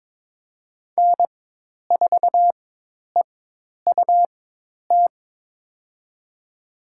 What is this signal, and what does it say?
Morse code, heard as a keyed tone, about 22 wpm, spaced as for 10 wpm, 700 Hz. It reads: N4EUT
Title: USA Call Signs - 5 Characters - 4x Character Spacing - 22wpm